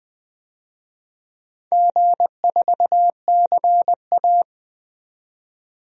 Morse code, heard as a keyed tone, about 20 wpm, 700 Hz. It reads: G4CA
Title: DXCC-100 Call Signs - 4 Characters - 20wpm